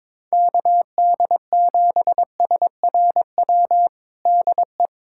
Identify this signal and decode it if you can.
Morse code, heard as a keyed tone, about 22 wpm, 700 Hz. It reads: KD7SRW DE
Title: USA Callsign DE Callsign - 6 Characters - 22wpm